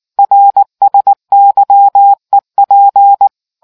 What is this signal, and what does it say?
Morse code, heard as a keyed tone, about 19 wpm, 800 Hz. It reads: RSYEP